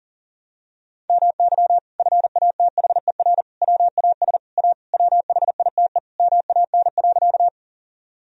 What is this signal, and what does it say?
Morse code, heard as a keyed tone, about 40 wpm, 700 Hz. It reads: MY FATHER WAS A WHITE MAN.
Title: Sentences from Top 1000 Words - Review Part 1 - 40wpm